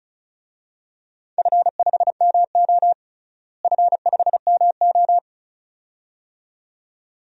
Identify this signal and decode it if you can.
Morse code, heard as a keyed tone, about 36 wpm, 700 Hz. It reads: F5MO F5MO
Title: DXCC-100 Call Signs - 4 Characters - Repeated 2x - 35wpm